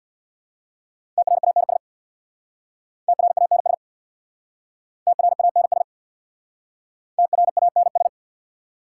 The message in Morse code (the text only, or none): TRANS TRANS TRANS TRANS